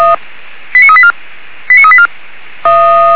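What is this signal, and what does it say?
Five-tone selective call (ZVEI), heard as quick runs of five tones, tones of about 70 ms. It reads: 89384 89384
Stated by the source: ZVEI five-tone call 89401 received by SDR, archived by feuerwehr-orlamuende (Orlamünde fire brigade).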